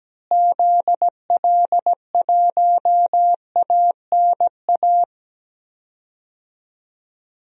Morse code, heard as a keyed tone, about 17 wpm, 700 Hz. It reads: ZL1ANA
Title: DXCC-100 Call Signs - 6 Characters - Repeated 3x - 17wpm